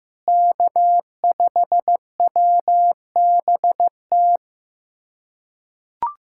Morse code, heard as a keyed tone, about 15 wpm, 700 Hz. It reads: K5WBT
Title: USA Callsign DE Callsign - 5 Characters - 15wpm